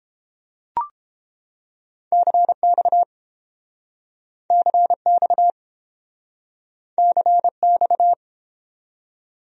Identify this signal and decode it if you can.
Morse code, heard as a keyed tone, about 30 wpm, 700 Hz. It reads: CX CX CX